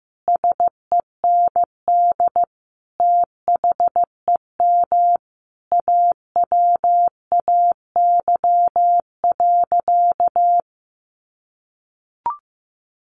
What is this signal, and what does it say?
Morse code, heard as a keyed tone, about 15 wpm, 700 Hz. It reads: SEND THEM AWAY.
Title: Sentences from Top 1500 Words 15wpm